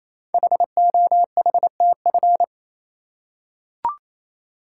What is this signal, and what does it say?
Morse code, heard as a keyed tone, about 28 wpm, 700 Hz. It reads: HOHTF
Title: Code Groups - 5 Letters - 28wpm